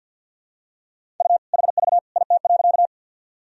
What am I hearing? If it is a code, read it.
Morse code, heard as a keyed tone, about 50 wpm, 700 Hz. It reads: U HV IT.
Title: Sentences from Top 100 Words and Abbreviations 50wpm